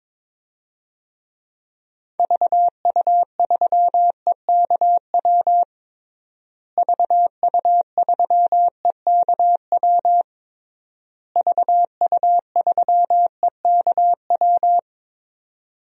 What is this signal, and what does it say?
Morse code, heard as a keyed tone, about 22 wpm, 700 Hz. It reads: VU3EKW VU3EKW VU3EKW
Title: DXCC-100 Call Signs - 6 Characters - Repeated 3x - 22wpm